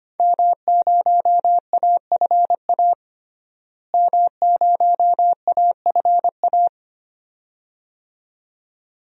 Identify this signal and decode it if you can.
Morse code, heard as a keyed tone, about 25 wpm, 700 Hz. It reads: M0AFA M0AFA
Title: DXCC-100 Call Signs - 5 Characters - Repeated 2x - 25wpm